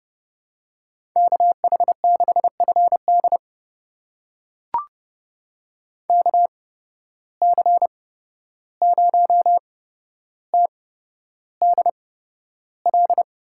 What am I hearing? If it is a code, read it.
Morse code, heard as a keyed tone, about 31 wpm, spaced as for 12 wpm, 700 Hz. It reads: KH6FD KC0TDL
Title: USA Call Signs - 8x Character Spacing - 30wpm